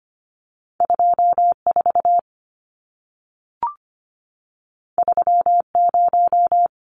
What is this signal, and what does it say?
Morse code, heard as a keyed tone, about 25 wpm, 700 Hz. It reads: 24 30
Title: Numbers - 2-Digit 25wpm